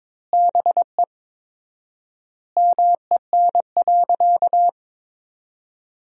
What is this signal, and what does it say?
Morse code, heard as a keyed tone, about 21 wpm, 700 Hz. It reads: BE MEN.
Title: 4x Word-Spacing - Sentences from Top 300 Words - 22wpm